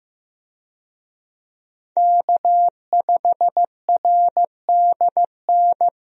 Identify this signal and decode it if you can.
Morse code, heard as a keyed tone, about 15 wpm, 700 Hz. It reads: K5RDN
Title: USA Call Signs - 5 Characters - Repeated 2x - 15wpm